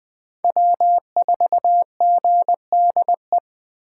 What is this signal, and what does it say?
Morse code, heard as a keyed tone, about 20 wpm, 700 Hz. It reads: W4GDE